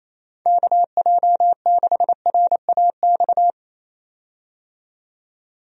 Morse code, heard as a keyed tone, about 28 wpm, 700 Hz. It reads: KJ6RAX
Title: USA Call Signs - 6 Characters - 28wpm